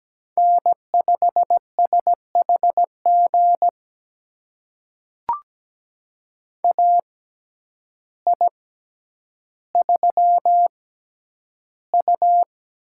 Morse code, heard as a keyed tone, about 17 wpm, spaced as for 9 wpm, 700 Hz. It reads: N5SHG AI3U
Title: USA Call Signs - 6x Character Spacing - 17wpm